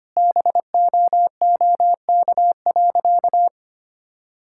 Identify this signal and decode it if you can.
Morse code, heard as a keyed tone, about 25 wpm, 700 Hz. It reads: BOOK.
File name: practice / 25wpm / Sentences with 2-Word Phrases - 25wpm